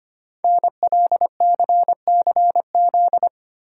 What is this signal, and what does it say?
Morse code, heard as a keyed tone, about 25 wpm, 700 Hz. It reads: NLCCZ
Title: Code Groups - 5 Letters - 25wpm